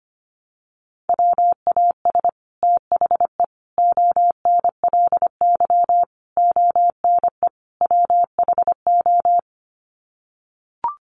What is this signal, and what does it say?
Morse code, heard as a keyed tone, about 25 wpm, 700 Hz. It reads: WAS THE ONLY ONE WHO
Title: Sets of 5 Words Encore2 25wpm